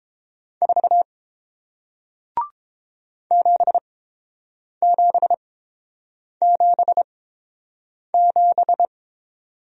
Morse code, heard as a keyed tone, about 30 wpm, spaced as for 7 wpm, 700 Hz. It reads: 4 7777